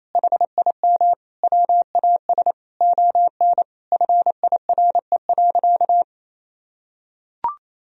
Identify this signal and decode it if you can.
Morse code, heard as a keyed tone, about 28 wpm, 700 Hz. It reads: HIM WAS ON FIRE.
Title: Sentences from Top 1000 Words - Review Part 3 - 28wpm